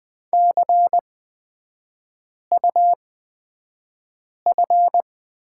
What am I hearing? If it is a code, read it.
Morse code, heard as a keyed tone, about 20 wpm, spaced as for 5 wpm, 700 Hz. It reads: CUF
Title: 255 Lesson 41e - Period - Three Characters 20wpm